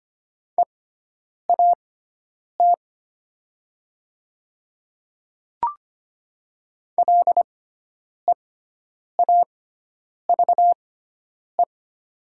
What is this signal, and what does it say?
Morse code, heard as a keyed tone, about 25 wpm, spaced as for 9 wpm, 700 Hz. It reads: EAT LEAVE
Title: 6x Character Spacing - Top 500 Words - 25wpm